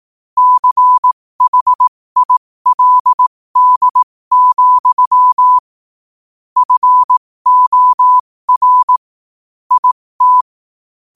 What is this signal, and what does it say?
Morse code, heard as a keyed tone, about 18 wpm, spaced as for 15 wpm, 1000 Hz. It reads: CHILD, FOR IT
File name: 15wpm